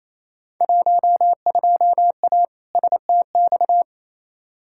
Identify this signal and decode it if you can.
Morse code, heard as a keyed tone, about 27 wpm, 700 Hz. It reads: 12A STX
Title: Field Day Exchanges Encore - 28wpm